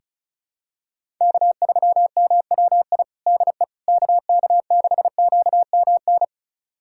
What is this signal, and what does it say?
Morse code, heard as a keyed tone, about 34 wpm, 700 Hz. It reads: K3MWI DE KK6QMN
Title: USA Callsign DE Callsign 35wpm